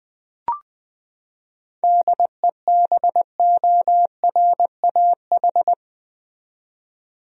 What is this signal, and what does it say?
Morse code, heard as a keyed tone, about 21 wpm, 700 Hz. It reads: DEBORAH
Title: USA Common Names - 20wpm